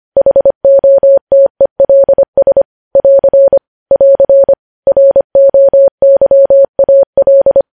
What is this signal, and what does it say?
Morse code, heard as a keyed tone, about 25 wpm, 550 Hz. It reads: HOTELS <AR> <AR> ROYAL
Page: News Headlines in Morse Code at 25 WPM